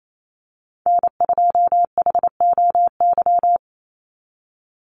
Morse code, heard as a keyed tone, about 28 wpm, 700 Hz. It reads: N2HOY
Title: USA Call Signs 28wpm